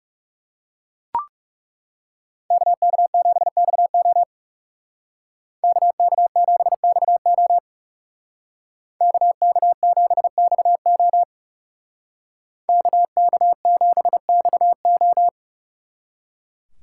Morse code, heard as a keyed tone, about 37 wpm, 700 Hz. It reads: KK7XO KK7XO KK7XO KK7XO
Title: Speed Racing - USA 2x2 Call Signs 30wpm